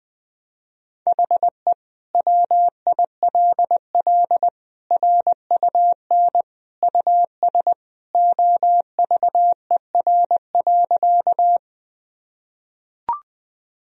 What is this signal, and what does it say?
Morse code, heard as a keyed tone, about 20 wpm, 700 Hz. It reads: HE WILL RUN US OVER.
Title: Sentences from Top 1000 Words - Review Part 1 - 20wpm